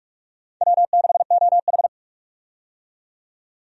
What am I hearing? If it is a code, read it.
Morse code, heard as a keyed tone, about 47 wpm, 700 Hz. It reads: W6OH
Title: USA Call Signs - 4 Characters - 45wpm